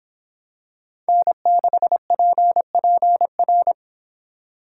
Morse code, heard as a keyed tone, about 25 wpm, 700 Hz. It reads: N6PPR